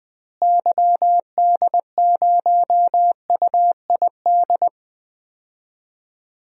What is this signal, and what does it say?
Morse code, heard as a keyed tone, about 20 wpm, 700 Hz. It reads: YD0UID
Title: DXCC-100 Call Signs - 6 Characters - Repeated 3x - 20wpm